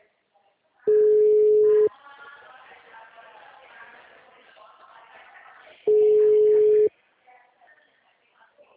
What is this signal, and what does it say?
Ringback tone (Europe, 425 Hz), with speech.
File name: out- -2014- - - .91884